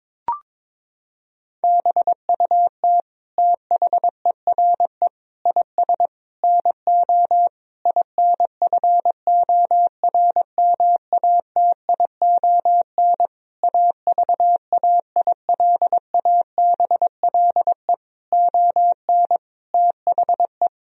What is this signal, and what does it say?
Morse code, heard as a keyed tone, about 22 wpm, 700 Hz. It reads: BUT THERE IS NO INFORMATION AVAILABLE ON THE
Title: Sentences from Top 1000 Words - Review Part 3 - 22wpm